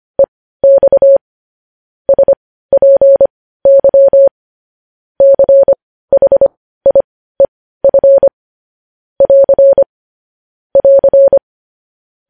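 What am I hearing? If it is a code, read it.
Morse code, heard as a keyed tone, about 25 wpm, spaced as for 15 wpm, 550 Hz. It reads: EX SPY CHIEF <AR> <AR>
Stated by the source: Morse code practice of news headlines at 15 wpm.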